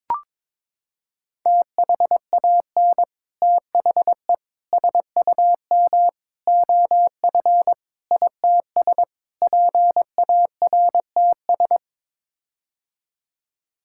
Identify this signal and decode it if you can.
Morse code, heard as a keyed tone, about 22 wpm, 700 Hz. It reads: THAN THE SUM OF ITS PARTS
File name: practice / 22wpm / Sets of 6 Words - 22wpm